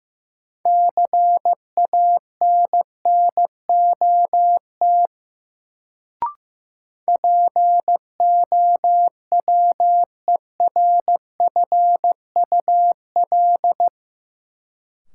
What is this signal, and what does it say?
Morse code, heard as a keyed tone, about 15 wpm, 700 Hz. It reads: CANNOT POWERFUL
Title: Sentences from Top 1000 Words - Review Part 6 - New Word Drill - 15wpm